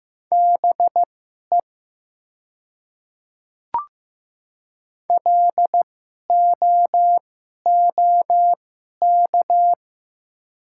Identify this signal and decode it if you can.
Morse code, heard as a keyed tone, about 15 wpm, spaced as for 11 wpm, 700 Hz. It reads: BE LOOK
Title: 2x Character Spacing - Top 100 Words - 15wpm